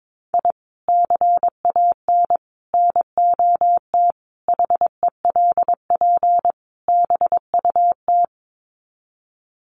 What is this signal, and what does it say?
Morse code, heard as a keyed tone, about 22 wpm, 700 Hz. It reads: I CAN NOT HELP BUT
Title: Sets of 5 Words Encore2 22wpm